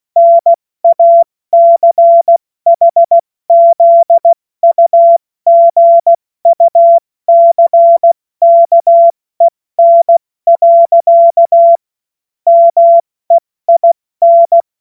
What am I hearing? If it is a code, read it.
Morse code, heard as a keyed tone, about 16 wpm, 680 Hz. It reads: NACHZUGUCKEN. MEIN
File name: Morsekurs / Witze / Witz 01. November 2022 70 F80